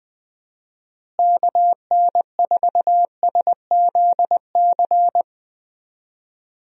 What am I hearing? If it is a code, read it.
Morse code, heard as a keyed tone, about 20 wpm, 700 Hz. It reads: KN4SZC
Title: USA Call Signs - 6 Characters - Repeated 2x - 20wpm